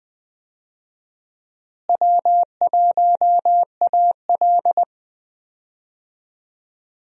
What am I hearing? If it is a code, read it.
Morse code, heard as a keyed tone, about 20 wpm, 700 Hz. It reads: W1AL
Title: USA Call Signs - 4 Characters - Repeated 2x - 20wpm